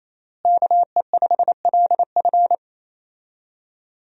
Morse code, heard as a keyed tone, about 28 wpm, 700 Hz. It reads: KE5LF
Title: USA Call Signs - 5 Characters - Repeated 2x - 28wpm